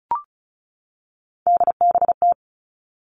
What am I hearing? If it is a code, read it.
Morse code, heard as a keyed tone, about 36 wpm, 700 Hz. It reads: DBT
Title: Three Challenging Letter-Number Combinations 35wpm